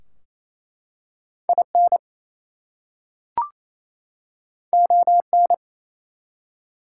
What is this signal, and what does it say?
Morse code, heard as a keyed tone, about 27 wpm, 700 Hz. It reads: IN ON
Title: Two Letter Words - 28wpm